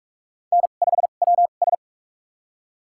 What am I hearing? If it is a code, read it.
Morse code, heard as a keyed tone, about 47 wpm, spaced as for 33 wpm, 700 Hz. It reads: N5WS
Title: USA Call Signs - 4 Characters - 2x Character Spacing - 45wpm